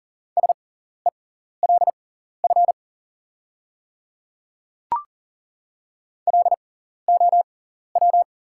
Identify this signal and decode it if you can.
Morse code, heard as a keyed tone, about 39 wpm, spaced as for 14 wpm, 690 Hz. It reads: SELF LOW
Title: 6x Character Spacing - Top 500 Words - 40wpm